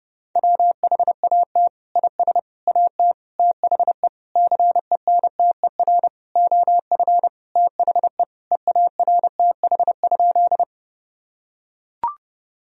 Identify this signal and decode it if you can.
Morse code, heard as a keyed tone, about 32 wpm, 700 Hz. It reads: WHAT IS AT THE CENTER OF THE EARTH?
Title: Sentences from Top 1000 Words - Review Part 2 - 30wpm